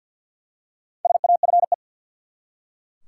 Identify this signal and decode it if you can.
Morse code, heard as a keyed tone, about 51 wpm, 700 Hz. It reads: SAFE
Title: Four Letter Words - 50wpm